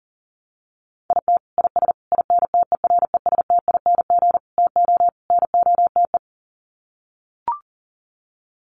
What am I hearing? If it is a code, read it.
Morse code, heard as a keyed tone, about 40 wpm, 700 Hz. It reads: IT IS INTERESTING TO NOTE
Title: Sets of 5 Words Encore2 40wpm